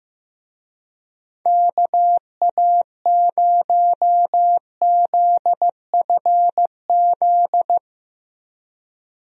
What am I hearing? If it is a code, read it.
Morse code, heard as a keyed tone, about 16 wpm, 700 Hz. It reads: KA0ZFZ
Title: USA Call Signs - 6 Characters - Repeated 2x - 15wpm